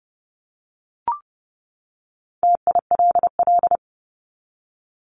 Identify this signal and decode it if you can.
Morse code, heard as a keyed tone, about 30 wpm, 700 Hz. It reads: TILL
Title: Top 1751-2000 Words 30wpm